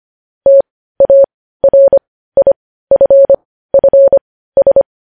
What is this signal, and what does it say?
Morse code, heard as a keyed tone, about 25 wpm, spaced as for 15 wpm, 550 Hz. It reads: TARIFFS
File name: podcast / News headlines-1x-25x15